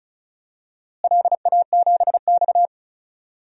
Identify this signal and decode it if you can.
Morse code, heard as a keyed tone, about 36 wpm, 700 Hz. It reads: LA7X
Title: DXCC-100 Call Signs - 4 Characters - Repeated 2x - 35wpm